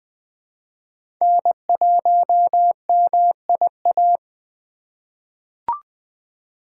Morse code, heard as a keyed tone, about 20 wpm, 700 Hz. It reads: N1MIA
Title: USA Call Signs - 2x Character Spacing - 20wpm